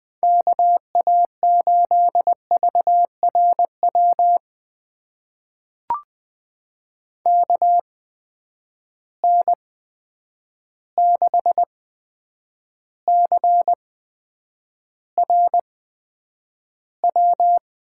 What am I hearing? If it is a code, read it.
Morse code, heard as a keyed tone, about 20 wpm, spaced as for 9 wpm, 700 Hz. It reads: KA8VRW KN6CRW
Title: USA Call Signs - 6 Characters - 8x Character Spacing - 20wpm